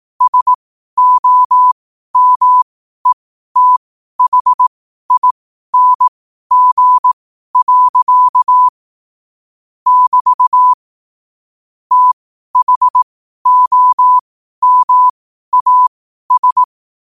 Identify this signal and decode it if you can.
Morse code, heard as a keyed tone, about 18 wpm, spaced as for 12 wpm, 1000 Hz. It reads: SOMETHING. <BT> THOMAS
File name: qotdfiles / 12wpm